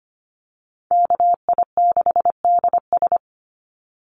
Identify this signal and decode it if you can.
Morse code, heard as a keyed tone, about 25 wpm, 700 Hz. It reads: KI6DS
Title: NA SOTA Chaser Callsigns 25wpm